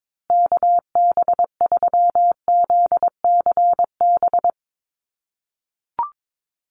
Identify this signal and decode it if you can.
Morse code, heard as a keyed tone, about 22 wpm, 700 Hz. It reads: KB3ZCB
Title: USA Call Signs Repeated 2x 22wpm